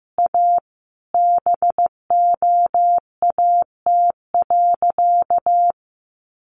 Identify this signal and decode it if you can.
Morse code, heard as a keyed tone, about 15 wpm, 700 Hz. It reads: A BOAT.
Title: Sentences from Top 1000 Words 15wpm